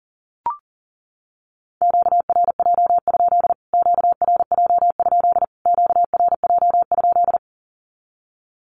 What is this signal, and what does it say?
Morse code, heard as a keyed tone, about 40 wpm, 700 Hz. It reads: QRJ? QRJ? QRJ?
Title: Q-Codes Repeated 3x 40wpm